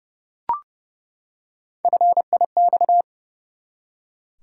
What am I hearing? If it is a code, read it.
Morse code, heard as a keyed tone, about 30 wpm, 700 Hz. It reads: FIX
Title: Three Letter Words - 30wpm